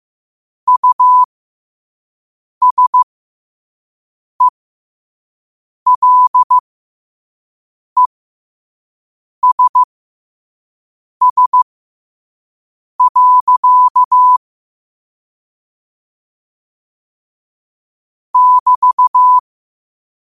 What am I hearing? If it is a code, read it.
Morse code, heard as a keyed tone, about 15 wpm, spaced as for 5 wpm, 1000 Hz. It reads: USELESS. <BT>
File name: qotdfiles / 5wpm